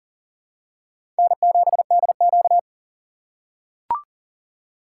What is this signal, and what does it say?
Morse code, heard as a keyed tone, about 40 wpm, 700 Hz. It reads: N7DQ
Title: USA Call Signs - 4 Characters - 4x Character Spacing - 40wpm